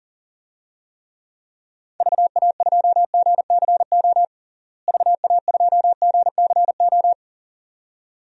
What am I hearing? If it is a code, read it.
Morse code, heard as a keyed tone, about 40 wpm, 700 Hz. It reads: VA2GCO VA2GCO
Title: DXCC-100 Call Signs - 6 Characters - Repeated 2x - 40wpm